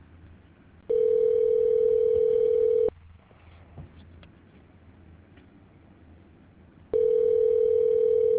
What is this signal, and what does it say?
Ringback tone (North America), 440 + 480 Hz.